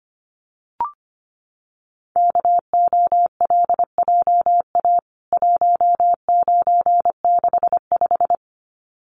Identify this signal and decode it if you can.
Morse code, heard as a keyed tone, about 25 wpm, 700 Hz. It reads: KOLJA 1965
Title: CWT Exchange 25wpm